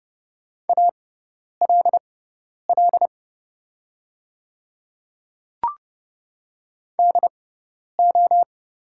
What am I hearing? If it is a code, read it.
Morse code, heard as a keyed tone, about 30 wpm, spaced as for 10 wpm, 700 Hz. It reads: ALL DO
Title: 6x Character Spacing - Top 300 Words - 30wpm